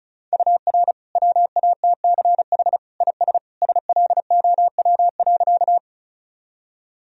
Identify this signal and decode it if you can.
Morse code, heard as a keyed tone, about 34 wpm, 700 Hz. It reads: UR WATCH IS SLOW.
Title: Sentences from Top 400 Words and Abbreviations 35wpm